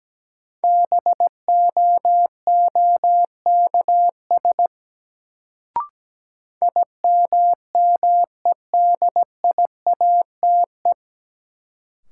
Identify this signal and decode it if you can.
Morse code, heard as a keyed tone, about 17 wpm, 700 Hz. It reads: BOOKS IMMEDIATE
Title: Sentences from Top 1200 Words - New Word Drill Alt - 17wpm